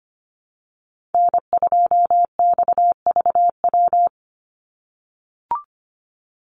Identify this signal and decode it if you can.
Morse code, heard as a keyed tone, about 25 wpm, 700 Hz. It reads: N2XVW
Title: USA Call Signs 25wpm